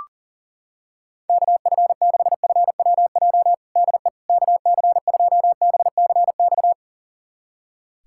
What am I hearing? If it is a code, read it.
Morse code, heard as a keyed tone, about 40 wpm, 700 Hz. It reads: KF6FWJ DE KC2BCX
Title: USA Callsign DE Callsign - 6 Characters - 40wpm